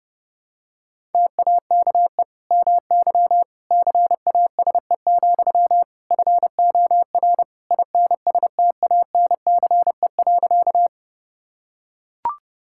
Morse code, heard as a keyed tone, about 30 wpm, 700 Hz. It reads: TAKE MY CASE, FOR INSTANCE.
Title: Sentences from Top 1000 Words - Review Part 5 - 30wpm